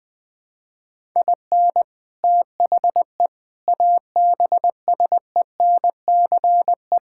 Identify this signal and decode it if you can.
Morse code, heard as a keyed tone, about 20 wpm, 700 Hz. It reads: IN THE ABSENCE
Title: Sentences with 4-Word Phrases - 20wpm